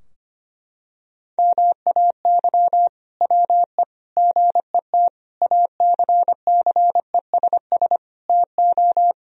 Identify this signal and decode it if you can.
Morse code, heard as a keyed tone, about 25 wpm, 700 Hz. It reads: MAY WE GET ACCESS TO
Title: Sentences from Top 1000 Words - Review Part 4 - 25wpm